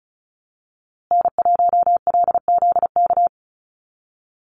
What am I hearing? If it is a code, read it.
Morse code, heard as a keyed tone, about 34 wpm, 700 Hz. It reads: N1LZK
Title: USA Call Signs Repeated 2x 35wpm